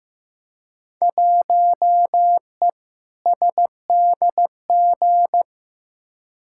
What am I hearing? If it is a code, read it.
Morse code, heard as a keyed tone, about 15 wpm, 700 Hz. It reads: 1E SDG
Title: Field Day Exchanges - 15wpm